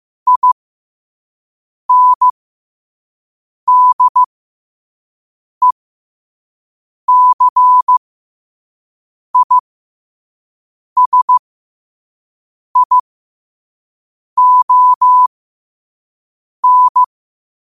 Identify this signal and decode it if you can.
Morse code, heard as a keyed tone, about 15 wpm, spaced as for 5 wpm, 1000 Hz. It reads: INDECISION
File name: qotdfiles / 5wpm